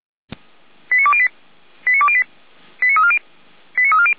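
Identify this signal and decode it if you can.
Five-tone selective call (ZVEI), heard as quick runs of five tones, tones of about 70 ms. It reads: 89208 89208 89340 89340
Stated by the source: Five-tone ZVEI call received on an SDR, fire brigade Feuerwehr Orlamünde.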